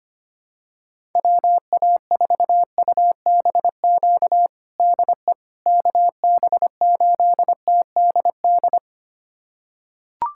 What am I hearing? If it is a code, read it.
Morse code, heard as a keyed tone, about 25 wpm, 700 Hz. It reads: WA4UBQ DE KB8TDD
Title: USA Callsign DE Callsign - 6 Characters - 25wpm